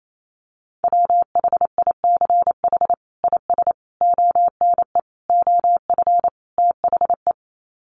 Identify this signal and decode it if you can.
Morse code, heard as a keyed tone, about 28 wpm, 700 Hz. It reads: WHICH IS ONE OF THE
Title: Sets of 5 Words Encore 28wpm